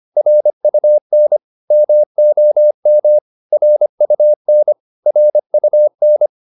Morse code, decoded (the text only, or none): RUN MOM RUN RUN